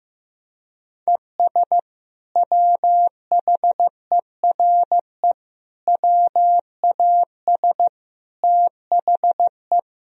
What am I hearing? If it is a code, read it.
Morse code, heard as a keyed tone, about 15 wpm, 700 Hz. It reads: ES WHERE WAS THE
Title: Sentences from Top 200 Words and Abbreviations 15wpm